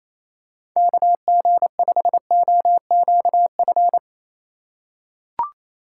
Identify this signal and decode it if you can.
Morse code, heard as a keyed tone, about 28 wpm, 700 Hz. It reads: KG5OQF
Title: USA Call Signs - 6 Characters - 2x Character Spacing - 28wpm